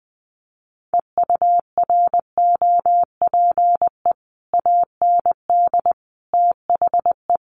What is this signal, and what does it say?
Morse code, heard as a keyed tone, about 20 wpm, 700 Hz. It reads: EUROPE AND THE
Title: Sets of 5 Words Encore2 20wpm